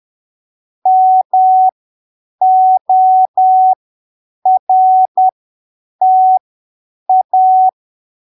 Morse code, heard as a keyed tone, about 10 wpm, spaced as for 7 wpm, 750 Hz. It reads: MORTA
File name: arquivos / texto / texto 20 10ppm